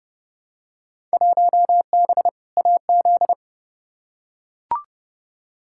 Morse code, heard as a keyed tone, about 30 wpm, 700 Hz. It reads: 1B AZ
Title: Field Day Exchanges Encore - 30wpm